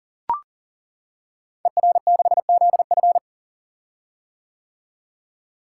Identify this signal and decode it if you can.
Morse code, heard as a keyed tone, about 40 wpm, 700 Hz. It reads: ER6ZF
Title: Code Groups - 5 Letter:Numbers - 40wpm